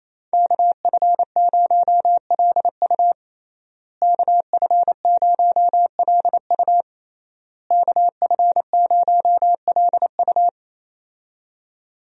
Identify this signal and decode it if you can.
Morse code, heard as a keyed tone, about 28 wpm, 700 Hz. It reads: KF0LU KF0LU KF0LU
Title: USA Call Signs - 5 Characters - Repeated 3x - 28wpm